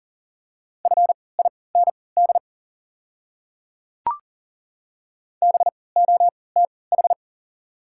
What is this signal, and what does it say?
Morse code, heard as a keyed tone, about 40 wpm, spaced as for 23 wpm, 690 Hz. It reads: FIND BOTH
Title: 3x Character Spacing - Top 500 Words - 40wpm